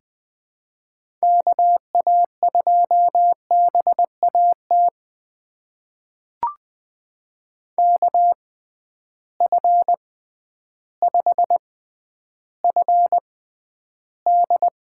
Morse code, heard as a keyed tone, about 20 wpm, spaced as for 11 wpm, 700 Hz. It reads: KA2BAT KF5FD
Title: USA Call Signs - 6 Characters - 6x Character Spacing - 20wpm